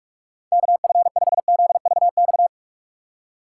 Morse code, heard as a keyed tone, about 46 wpm, 700 Hz. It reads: KF5ZVX
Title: USA Call Signs - 6x Character Spacing - 45wpm